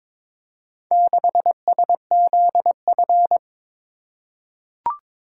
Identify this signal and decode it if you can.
Morse code, heard as a keyed tone, about 22 wpm, 700 Hz. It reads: 6SZF
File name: practice / 22wpm / Four Letter-Number Combinations - 22wpm